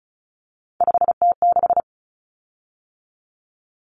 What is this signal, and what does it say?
Morse code, heard as a keyed tone, about 34 wpm, 700 Hz. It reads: 5T6
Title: Common Cut Numbers - 3-Digit 35wpm